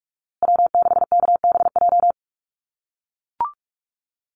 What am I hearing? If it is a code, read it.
Morse code, heard as a keyed tone, about 44 wpm, 690 Hz. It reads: W6KBJ
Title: USA Call Signs Repeated 3x 45wpm